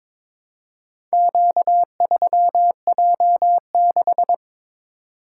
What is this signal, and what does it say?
Morse code, heard as a keyed tone, about 22 wpm, 700 Hz. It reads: Q3J6
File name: practice / 22wpm / Four Letter-Number Combinations - 22wpm